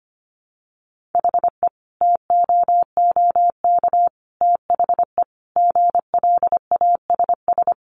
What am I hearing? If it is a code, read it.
Morse code, heard as a keyed tone, about 25 wpm, 700 Hz. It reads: HE TOOK THE GLASS
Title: Sentences from Top 1300 Words 25wpm